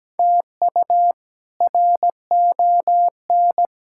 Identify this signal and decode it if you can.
Morse code, heard as a keyed tone, about 17 wpm, 700 Hz. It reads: TU RON
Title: K1USN SST QSO Practice - 17wpm